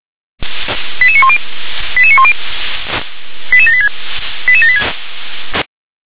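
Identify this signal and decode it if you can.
Five-tone selective call (ZVEI), heard as quick runs of five tones, tones of about 70 ms. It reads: 88910 88910 88786 88786
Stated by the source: ZVEI five-tone selective call received on SDR, address 88792.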